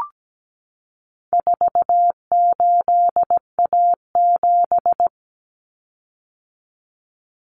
Morse code, heard as a keyed tone, about 17 wpm, 700 Hz. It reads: 48A7
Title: Common Cut Numbers - 4-Digit 17wpm